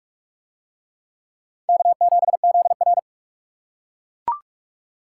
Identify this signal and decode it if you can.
Morse code, heard as a keyed tone, about 40 wpm, 700 Hz. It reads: K7ZR